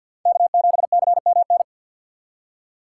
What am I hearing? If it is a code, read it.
Morse code, heard as a keyed tone, about 63 wpm, 700 Hz. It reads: K7CMN